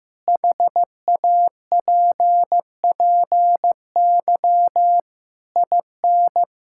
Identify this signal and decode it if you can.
Morse code, heard as a keyed tone, about 15 wpm, 700 Hz. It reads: HAPPY IN
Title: Sentences from Top 1600 Words 15wpm